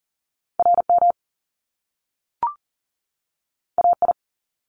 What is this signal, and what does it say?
Morse code, heard as a keyed tone, about 38 wpm, 700 Hz. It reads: RM AI
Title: Top Two Letter Combinations in Words 40wpm